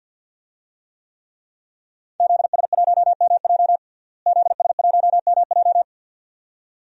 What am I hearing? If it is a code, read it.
Morse code, heard as a keyed tone, about 51 wpm, 700 Hz. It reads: ZS1MJ ZS1MJ
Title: DXCC-100 Call Signs - 5 Characters - Repeated 2x - 50wpm